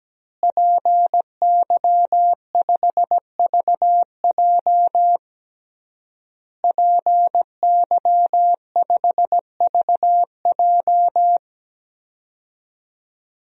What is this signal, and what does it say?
Morse code, heard as a keyed tone, about 17 wpm, 700 Hz. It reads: PY5VJ PY5VJ
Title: DXCC-100 Call Signs - 5 Characters - Repeated 3x - 17wpm